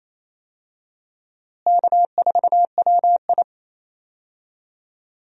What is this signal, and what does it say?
Morse code, heard as a keyed tone, about 28 wpm, 700 Hz. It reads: K4WI
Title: USA Call Signs - 4 Characters - Repeated 2x - 28wpm